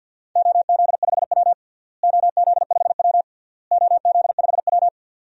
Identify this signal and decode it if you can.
Morse code, heard as a keyed tone, about 50 wpm, 700 Hz. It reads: OZ5W OZ5W OZ5W
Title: DXCC-100 Call Signs - 4 Characters - Repeated 3x - 50wpm